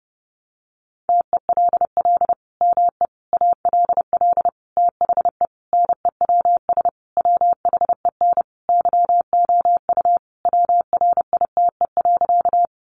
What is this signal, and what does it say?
Morse code, heard as a keyed tone, about 30 wpm, 700 Hz. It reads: TELL ME ALL THE NEWS WHEN YOU WRITE.
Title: Sentences from Top 800 Words 30wpm